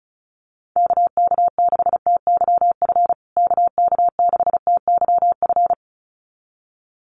Morse code, heard as a keyed tone, about 34 wpm, 700 Hz. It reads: KK6TYF KK6TYF
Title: USA Call Signs Repeated 2x 35wpm